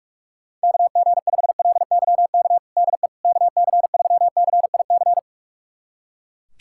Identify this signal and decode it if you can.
Morse code, heard as a keyed tone, about 42 wpm, 700 Hz. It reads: KG5LYK DE KC3CIC
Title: USA Callsign DE Callsign - 6 Characters - 45wpm